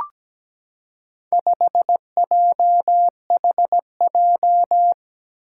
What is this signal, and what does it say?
Morse code, heard as a keyed tone, about 17 wpm, 700 Hz. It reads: 5JHJ
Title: Four Challenging Letter-Number Combinations 17wpm